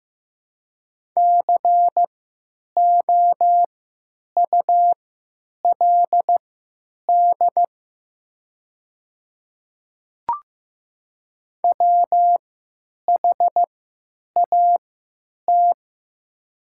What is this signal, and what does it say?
Morse code, heard as a keyed tone, about 15 wpm, spaced as for 9 wpm, 700 Hz. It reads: COULD WHAT
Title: 3x Character Spacing - Top 100 Words - 15wpm